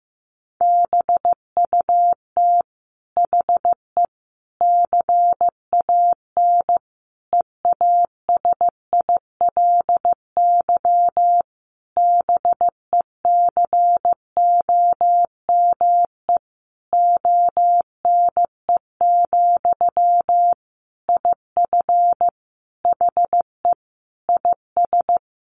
Morse code, heard as a keyed tone, about 15 wpm, 700 Hz. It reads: BUT HE CAN EASILY BECOME ONE, IF HE IS
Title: Sentences from Top 900 Words 15wpm